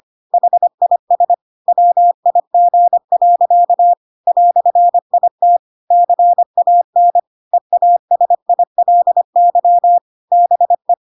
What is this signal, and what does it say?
Morse code, heard as a keyed tone, about 25 wpm, 700 Hz. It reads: HIS WIG. "IT CAN EASILY BE
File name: Pinocchio / Pinocchio ch1 25 25 ik1hge